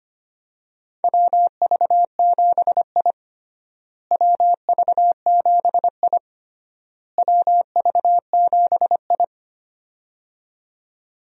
Morse code, heard as a keyed tone, about 25 wpm, 700 Hz. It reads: WV7I WV7I WV7I
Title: USA Call Signs - 4 Characters - Repeated 3x - 25wpm